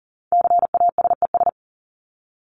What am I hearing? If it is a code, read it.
Morse code, heard as a keyed tone, about 40 wpm, 700 Hz. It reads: CASES
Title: ES:S Suffix - Top 100 Words with Base Words 40wpm